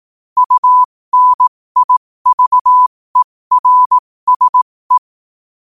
Morse code, heard as a keyed tone, about 18 wpm, spaced as for 15 wpm, 1000 Hz. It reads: UNIVERSE
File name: qotdfiles / 15wpm